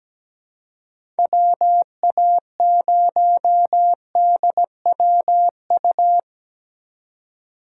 Morse code, heard as a keyed tone, about 17 wpm, 700 Hz. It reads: WA0DWU
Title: USA Call Signs - 6 Characters - 3x Character Spacing - 17wpm